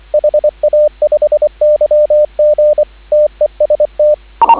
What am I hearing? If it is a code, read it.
Morse code, heard as a keyed tone, about 25 wpm, 590 Hz, with radio noise. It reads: HA5YG TEST